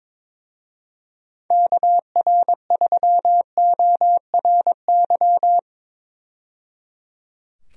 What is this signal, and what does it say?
Morse code, heard as a keyed tone, about 22 wpm, 700 Hz. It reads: KR3ORY
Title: USA Call Signs - 6 Characters - 22wpm